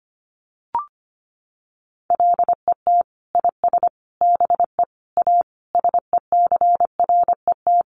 Morse code, heard as a keyed tone, about 25 wpm, 700 Hz. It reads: LET IS BE A SECRET
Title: Sentences from Top 1200 Words 25wpm